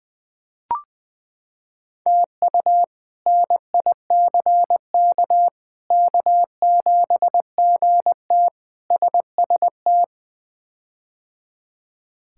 Morse code, heard as a keyed tone, about 20 wpm, 700 Hz. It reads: TU NICK K7GT SST
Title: K1USN SST QSO Practice - 20wpm